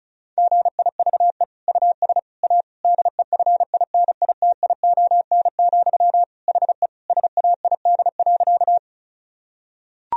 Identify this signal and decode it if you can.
Morse code, heard as a keyed tone, about 36 wpm, 700 Hz. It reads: GIVE US A DEFINITION, HE SAID.
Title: Sentences from Top 1000 Words - Review Part 4 - 35wpm